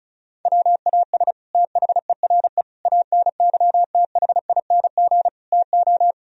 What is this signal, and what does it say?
Morse code, heard as a keyed tone, about 36 wpm, 700 Hz. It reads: WAS THERE ANYTHING TO
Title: Sentences from Top 1000 Words - Review Part 1 - 35wpm